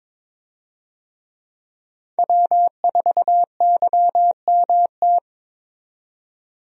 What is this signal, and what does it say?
Morse code, heard as a keyed tone, about 22 wpm, 700 Hz. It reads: W4YMT
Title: USA Call Signs - 5 Characters - 22wpm